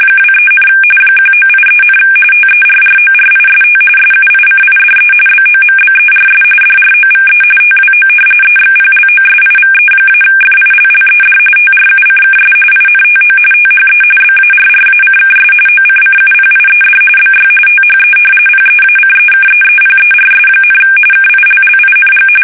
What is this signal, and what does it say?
RTTY, about 45.45 baud. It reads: 23, 1983, TERRORIST SUICIDE BOMBERS DROVE TWO EXPLOSIVE-FILLED TRUCKS INTO THE BARRACKS OCCUPIED BY THE PEACEKEEPERS, KILLING
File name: fsk850 / battleshipnj